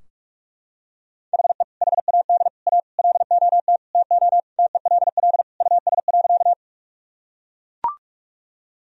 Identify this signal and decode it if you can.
Morse code, heard as a keyed tone, about 41 wpm, 700 Hz. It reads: HE HAD A LOT TO TELL US.
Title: Sentences from Top 1000 Words - Review Part 3 - 45wpm